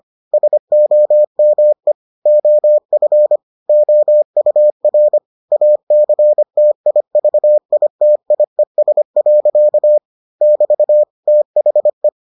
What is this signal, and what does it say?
Morse code, heard as a keyed tone, about 25 wpm, 600 Hz. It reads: SOME OF OUR ACTIVITIES. <BT> THE